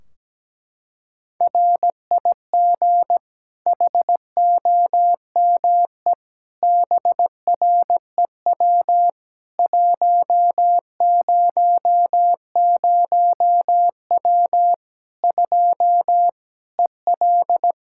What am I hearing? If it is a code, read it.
Morse code, heard as a keyed tone, about 17 wpm, 700 Hz. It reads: RIG HOME BREW 100W 2 EL
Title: Practice QSO - Exchange 2 - Info - 17wpm